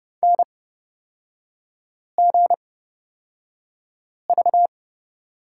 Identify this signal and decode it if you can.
Morse code, heard as a keyed tone, about 31 wpm, spaced as for 5 wpm, 690 Hz. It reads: NGV
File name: Lesson 38e - Letter q - Three Characters 30wpm